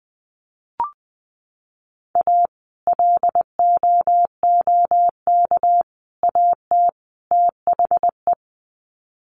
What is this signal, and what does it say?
Morse code, heard as a keyed tone, about 20 wpm, 700 Hz. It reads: A LOOK AT THE
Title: Sets of 4 Words 20wpm